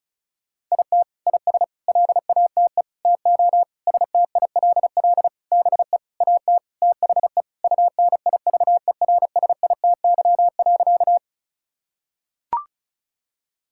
Morse code, heard as a keyed tone, about 34 wpm, 700 Hz. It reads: IT IS LATE TO STILL BE AT THE UNIVERSITY.
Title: Sentences from Top 1000 Words - Review Part 5 - 35wpm